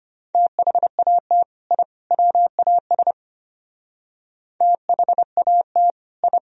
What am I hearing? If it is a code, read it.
Morse code, heard as a keyed tone, about 33 wpm, 700 Hz. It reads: THAT I WAS THAT I